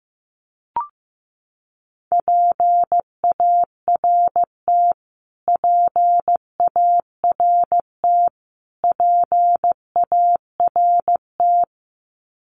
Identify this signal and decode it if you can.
Morse code, heard as a keyed tone, about 15 wpm, 700 Hz. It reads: PART PART PART
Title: Top 500 Words Repeated 15wpm